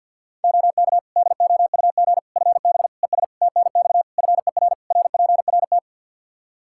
Morse code, heard as a keyed tone, about 52 wpm, 700 Hz. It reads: OK DOUG FB ES TNX FER RPRT
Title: Practice QSO - Exchange 2 - Info - 50wpm